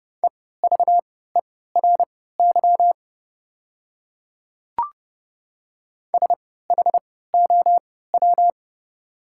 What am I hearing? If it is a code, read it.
Morse code, heard as a keyed tone, about 30 wpm, spaced as for 17 wpm, 700 Hz. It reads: EVERY SHOW
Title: 3x Character Spacing - Top 200 Words - 30wpm